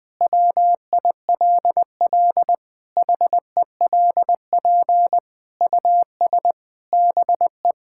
Morse code, heard as a keyed tone, about 20 wpm, 700 Hz. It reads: WILL HELP US BE
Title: Sentences with 7-Word Phrases - Spoken Beginning - 20wpm